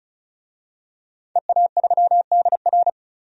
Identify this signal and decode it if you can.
Morse code, heard as a keyed tone, about 34 wpm, 700 Hz. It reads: EA3DR
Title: DXCC-100 Call Signs - 5 Characters - 35wpm